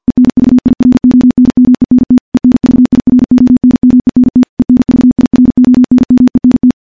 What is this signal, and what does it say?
Morse code, heard as a keyed tone, about 52 wpm, 250 Hz. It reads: ENVIRONMENT ENVIRONMENT ENVIRONMENT
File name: morse / morse-training-50WPM